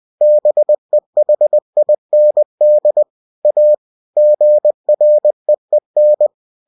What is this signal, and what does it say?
Morse code, heard as a keyed tone, about 20 wpm, 600 Hz. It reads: BEHIND A GREEN
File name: cw-books / winnie-the-pooh / winnie-20wpm-page-0004